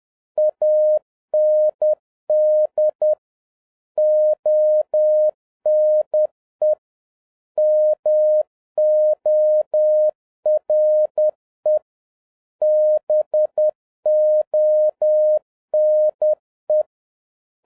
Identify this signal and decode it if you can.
Morse code, heard as a keyed tone, about 10 wpm, 610 Hz. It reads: AND ONE MORE BONE
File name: 2011-07-mp3 / n56 BACK FROM A BIG ACCIDENT 10wpm